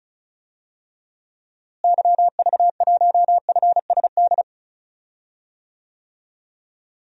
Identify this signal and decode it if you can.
Morse code, heard as a keyed tone, about 34 wpm, 700 Hz. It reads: YV1FSD
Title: DXCC-100 Call Signs - 6 Characters - Repeated 2x - 35wpm